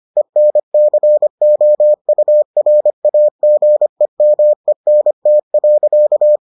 Morse code, heard as a keyed tone, about 25 wpm, 600 Hz. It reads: ENCOURAGEMENT.